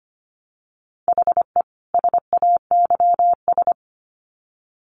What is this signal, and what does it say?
Morse code, heard as a keyed tone, about 25 wpm, 700 Hz. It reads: HE SAYS
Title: Sets of 2 Words 25wpm